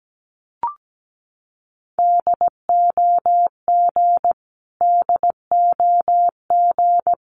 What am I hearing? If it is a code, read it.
Morse code, heard as a keyed tone, about 17 wpm, 700 Hz. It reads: DOG DOG
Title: Top 500 Words Repeated 17wpm